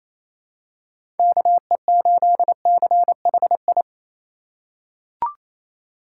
Morse code, heard as a keyed tone, about 28 wpm, 700 Hz. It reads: KE8CHI
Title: USA Call Signs - 6 Characters - 3x Character Spacing - 28wpm